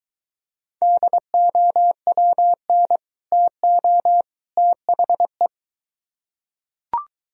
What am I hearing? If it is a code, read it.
Morse code, heard as a keyed tone, about 23 wpm, 700 Hz. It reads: DOWN TO THE